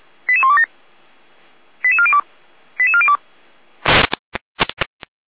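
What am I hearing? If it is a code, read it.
Five-tone selective call (ZVEI), heard as quick runs of five tones, tones of about 70 ms. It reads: 88127 88402 88402
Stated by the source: ZVEI five-tone call received by SDR, from the feuerwehr-orlamuende.org fire brigade audio server.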